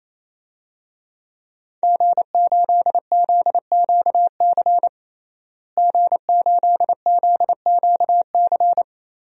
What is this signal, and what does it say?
Morse code, heard as a keyed tone, about 28 wpm, 700 Hz. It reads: G8ZQC G8ZQC
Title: DXCC-100 Call Signs - 5 Characters - Repeated 3x - 28wpm